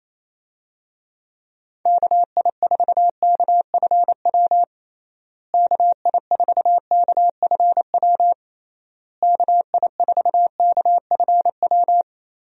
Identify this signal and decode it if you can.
Morse code, heard as a keyed tone, about 28 wpm, 700 Hz. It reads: KI4KFW KI4KFW KI4KFW
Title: USA Call Signs - 6 Characters - Repeated 3x - 28wpm